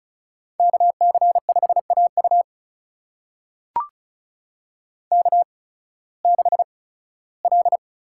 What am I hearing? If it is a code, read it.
Morse code, heard as a keyed tone, about 36 wpm, spaced as for 19 wpm, 700 Hz. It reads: KC5AU K6L
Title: USA Call Signs - 5 Characters - 8x Character Spacing - 35wpm